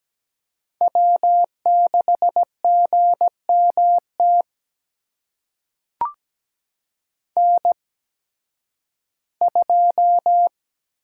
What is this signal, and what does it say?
Morse code, heard as a keyed tone, about 17 wpm, spaced as for 11 wpm, 700 Hz. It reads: W6GMT N2
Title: USA Call Signs - 5 Characters - 8x Character Spacing - 17wpm